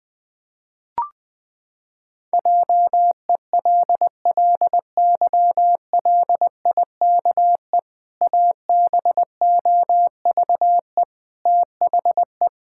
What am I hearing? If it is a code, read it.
Morse code, heard as a keyed tone, about 20 wpm, 700 Hz. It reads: JELLYLIKE ABOVE THE
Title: Pangram Challenge v2 - 20wpm